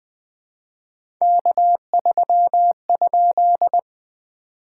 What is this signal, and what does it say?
Morse code, heard as a keyed tone, about 20 wpm, 700 Hz. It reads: K3?
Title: POTA QSO Practice - 20wpm